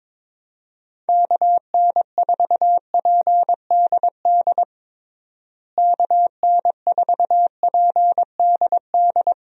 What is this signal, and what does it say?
Morse code, heard as a keyed tone, about 22 wpm, 700 Hz. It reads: KN4PDD KN4PDD
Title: USA Call Signs - 6 Characters - Repeated 2x - 22wpm